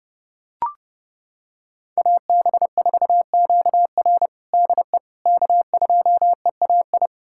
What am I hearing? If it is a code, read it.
Morse code, heard as a keyed tone, about 32 wpm, 700 Hz. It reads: AB4QR DE K2EAI
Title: USA Callsign DE Callsign - 5 Characters - 30wpm